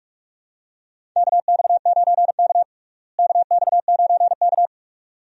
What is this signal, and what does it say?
Morse code, heard as a keyed tone, about 46 wpm, 700 Hz. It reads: KX9K KX9K
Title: USA Call Signs - 4 Characters - Repeated 2x - 45wpm